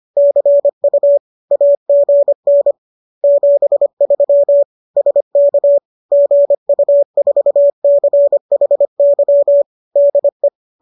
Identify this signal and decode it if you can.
Morse code, heard as a keyed tone, about 25 wpm, 570 Hz. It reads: CU AGN 73 SK GU4CHY DE
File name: qso210 25